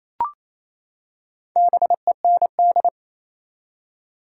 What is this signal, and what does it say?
Morse code, heard as a keyed tone, about 28 wpm, 700 Hz. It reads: BEND
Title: Four Letter Words - 28wpm